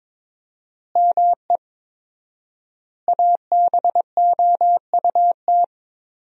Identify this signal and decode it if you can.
Morse code, heard as a keyed tone, about 22 wpm, 700 Hz. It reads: ME ABOUT
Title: 4x Word-Spacing - Sentences from Top 200 Words - 22wpm